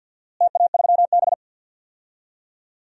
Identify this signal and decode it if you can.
Morse code, heard as a keyed tone, about 52 wpm, 700 Hz. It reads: TA3B
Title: DXCC-100 Call Signs - 4 Characters - Repeated 3x - 50wpm